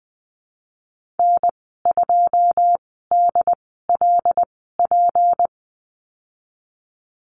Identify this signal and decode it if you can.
Morse code, heard as a keyed tone, about 20 wpm, spaced as for 15 wpm, 700 Hz. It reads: N2DLP
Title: 173 Lesson 28g - Number 9 - Call Signs 20wpm